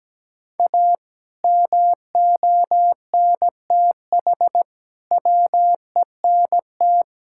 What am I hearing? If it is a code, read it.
Morse code, heard as a keyed tone, about 17 wpm, 700 Hz. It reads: A MONTH WENT
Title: Sentences from Top 1000 Words - Review Part 5 - 17wpm